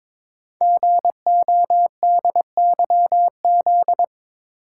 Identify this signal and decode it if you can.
Morse code, heard as a keyed tone, about 23 wpm, 700 Hz. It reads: GODYZ
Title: Code Groups - 5 Letters - 22wpm